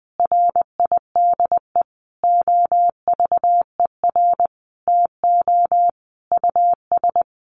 Morse code, heard as a keyed tone, about 20 wpm, 700 Hz. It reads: RIDE OVER TO US
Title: Sentences from Top 1200 Words 20wpm